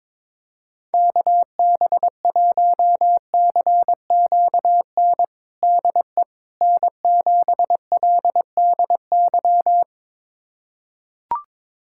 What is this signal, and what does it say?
Morse code, heard as a keyed tone, about 22 wpm, 700 Hz. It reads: KB1CQN DE N7LDY
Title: USA Callsign DE Callsign 22wpm